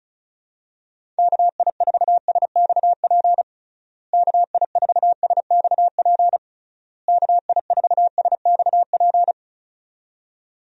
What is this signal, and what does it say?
Morse code, heard as a keyed tone, about 34 wpm, 700 Hz. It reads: KI4SXP KI4SXP KI4SXP
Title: USA Call Signs - 6 Characters - Repeated 3x - 35wpm